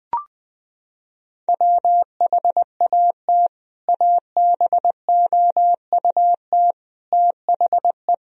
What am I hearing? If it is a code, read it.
Morse code, heard as a keyed tone, about 20 wpm, 700 Hz. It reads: WHAT ABOUT THE
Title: Sentences from Top 1000 Words - Review Part 2 - 20wpm